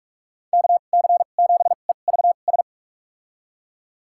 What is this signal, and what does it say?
Morse code, heard as a keyed tone, about 47 wpm, spaced as for 33 wpm, 700 Hz. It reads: KC7EVS
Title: USA Call Signs - 2x Character Spacing - 45wpm